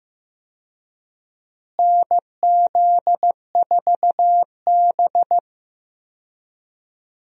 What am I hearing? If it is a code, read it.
Morse code, heard as a keyed tone, about 16 wpm, 700 Hz. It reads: NZ4B
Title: USA Call Signs - 4 Characters - Repeated 3x - 15wpm